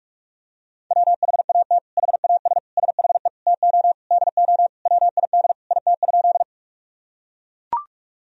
Morse code, heard as a keyed tone, about 42 wpm, 700 Hz. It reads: WHAT HAS SHE TO DO WID IT?
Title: Sentences from Top 100 Words and Abbreviations 45wpm